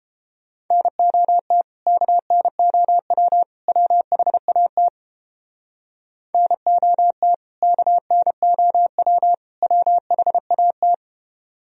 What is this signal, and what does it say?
Morse code, heard as a keyed tone, about 30 wpm, 700 Hz. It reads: NOT KNOW WHAT NOT KNOW WHAT